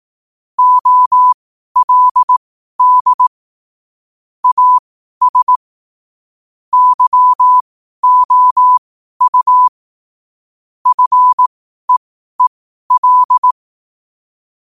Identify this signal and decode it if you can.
Morse code, heard as a keyed tone, about 18 wpm, spaced as for 12 wpm, 1000 Hz. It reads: OLD AS YOU FEEL
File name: qotdfiles / 12wpm